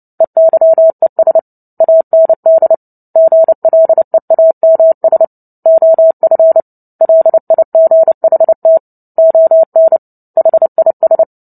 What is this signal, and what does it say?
Morse code, heard as a keyed tone, about 30 wpm, 650 Hz. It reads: EYES AND GLEAMS OF LIGHT ON HIS
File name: Book Two / 03-Chapter 3 - 29 WPM